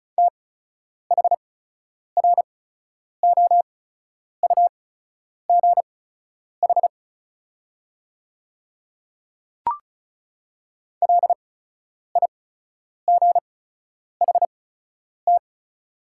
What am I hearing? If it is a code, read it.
Morse code, heard as a keyed tone, about 34 wpm, spaced as for 10 wpm, 700 Hz. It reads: THROUGH LIGHT
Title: 8x Character Spacing - Top 200 Words - 35wpm